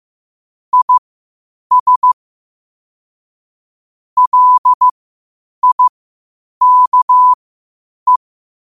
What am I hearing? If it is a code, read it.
Morse code, heard as a keyed tone, about 15 wpm, spaced as for 8 wpm, 1000 Hz. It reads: IS LIKE
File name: qotdfiles / 8wpm